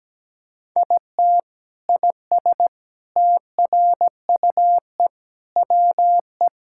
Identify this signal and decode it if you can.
Morse code, heard as a keyed tone, about 17 wpm, 700 Hz. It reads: IT IS TRUE WE
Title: Sentences from Top 1000 Words - Review Part 5 - 17wpm